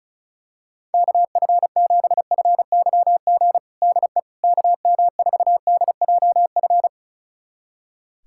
Morse code, heard as a keyed tone, about 36 wpm, 700 Hz. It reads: KF7FYG DE KM4DJF
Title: USA Callsign DE Callsign 35wpm